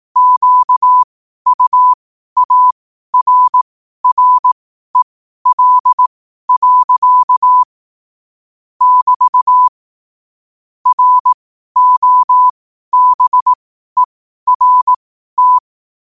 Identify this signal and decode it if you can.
Morse code, heard as a keyed tone, about 18 wpm, spaced as for 12 wpm, 1000 Hz. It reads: QUARREL. <BT> ROBERT
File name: qotdfiles / 12wpm